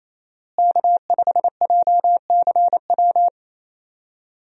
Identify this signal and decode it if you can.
Morse code, heard as a keyed tone, about 28 wpm, 700 Hz. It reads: K5JCW
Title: USA Call Signs - 5 Characters - Repeated 3x - 28wpm